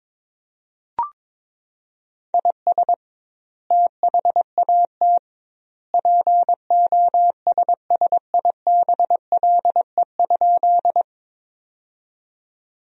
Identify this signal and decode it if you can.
Morse code, heard as a keyed tone, about 22 wpm, 700 Hz. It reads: IS THAT POSSIBLE?
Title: 2x Word-Spacing - Sentences from Top 500 Words - 22wpm